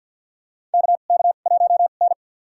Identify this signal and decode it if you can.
Morse code, heard as a keyed tone, about 47 wpm, spaced as for 35 wpm, 700 Hz. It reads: KK1N
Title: USA Call Signs - 4 Characters - 2x Character Spacing - 50wpm